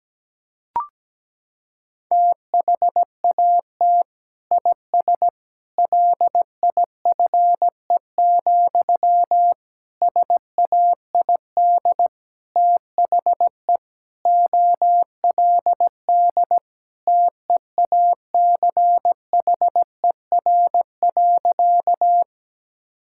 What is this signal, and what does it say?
Morse code, heard as a keyed tone, about 17 wpm, 700 Hz. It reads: THAT IS LIFE, SAID THE OLD TEACHER.
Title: Sentences from Top 1000 Words - Review Part 6 - 17wpm